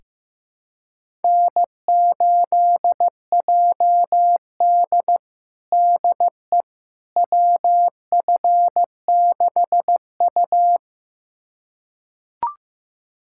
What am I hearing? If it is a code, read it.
Morse code, heard as a keyed tone, about 15 wpm, 700 Hz. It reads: N8JD DE WF6U
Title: USA Callsign DE Callsign - 4 Characters - 15wpm